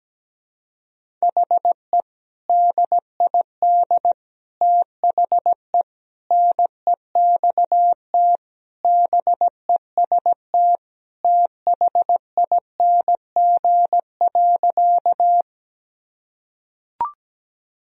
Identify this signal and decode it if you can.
Morse code, heard as a keyed tone, about 17 wpm, 700 Hz. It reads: HE DID THE NEXT BEST THING.
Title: Sentences from Top 1000 Words - Review Part 1 - 17wpm